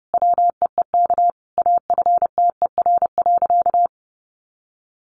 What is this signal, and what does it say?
Morse code, heard as a keyed tone, about 30 wpm, 700 Hz. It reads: WEEK AFTER.
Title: Sentences from Top 500 Words 30wpm